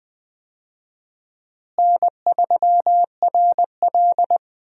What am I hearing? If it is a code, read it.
Morse code, heard as a keyed tone, about 21 wpm, 700 Hz. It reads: N3RL
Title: USA Call Signs - 4 Characters - Repeated 3x - 20wpm